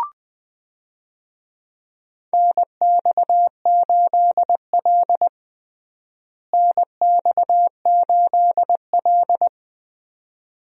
Morse code, heard as a keyed tone, about 20 wpm, 700 Hz. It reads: NX8L NX8L
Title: USA Call Signs - 4 Characters - Repeated 2x - 20wpm